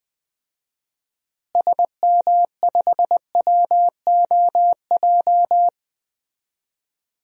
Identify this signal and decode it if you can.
Morse code, heard as a keyed tone, about 21 wpm, 700 Hz. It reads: SM5WOJ
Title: DXCC-100 Call Signs - 6 Characters - Repeated 3x - 20wpm